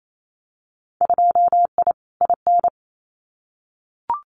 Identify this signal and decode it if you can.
Morse code, heard as a keyed tone, about 28 wpm, 700 Hz. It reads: 2I IN
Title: Winter Field Day Exchanges Encore 28wpm